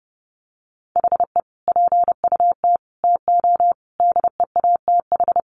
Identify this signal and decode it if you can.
Morse code, heard as a keyed tone, about 30 wpm, 700 Hz. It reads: HE PUT TO DEATH
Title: Sentences from Top 1500 Words 30wpm